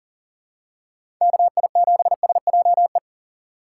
Morse code, heard as a keyed tone, about 40 wpm, 700 Hz. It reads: KI7SJE
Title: USA Call Signs - 6 Characters - 2x Character Spacing - 40wpm